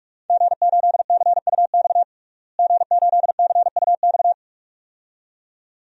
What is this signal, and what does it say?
Morse code, heard as a keyed tone, about 46 wpm, 700 Hz. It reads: G8CUX G8CUX
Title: DXCC-100 Call Signs - 5 Characters - Repeated 2x - 45wpm